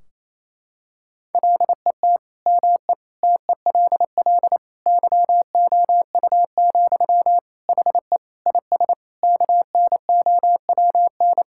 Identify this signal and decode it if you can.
Morse code, heard as a keyed tone, about 28 wpm, 700 Hz. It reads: LET ME TELL YOU, HE IS KNOWN
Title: Sentences from Top 1600 Words 28wpm